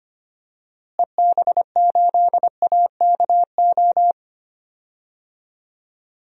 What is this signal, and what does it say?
Morse code, heard as a keyed tone, about 25 wpm, 700 Hz. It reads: EB8AKO
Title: DXCC-100 Call Signs - 6 Characters - Repeated 3x - 25wpm